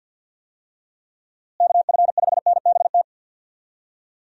Morse code, heard as a keyed tone, about 51 wpm, 700 Hz. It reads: KF5NBT
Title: USA Call Signs - 6 Characters - Repeated 3x - 50wpm